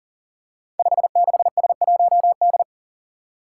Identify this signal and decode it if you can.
Morse code, heard as a keyed tone, about 38 wpm, 700 Hz. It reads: 56S1D
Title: Five Challenging Letter-Number Combinations 40wpm